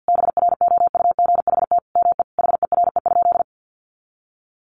Morse code, heard as a keyed tone, about 48 wpm, 700 Hz. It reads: BROUGHT ME HERE?
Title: Sentences from Top 1200 Words 50wpm